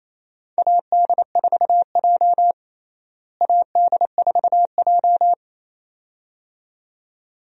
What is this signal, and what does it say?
Morse code, heard as a keyed tone, about 28 wpm, 700 Hz. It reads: AD4J AD4J
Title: USA Call Signs - 4 Characters - Repeated 2x - 28wpm